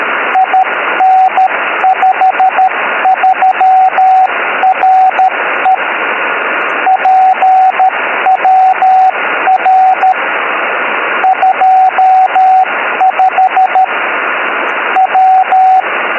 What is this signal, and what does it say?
Morse code, heard as a keyed tone, about 13 wpm, 750 Hz, with radio noise. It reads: IN53RE PWR 25 W